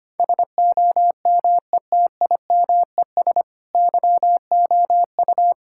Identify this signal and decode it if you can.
Morse code, heard as a keyed tone, about 25 wpm, 700 Hz. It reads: SOMETIMES YOU
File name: practice / 25wpm / Sentences with 3-Word Phrases - 25wpm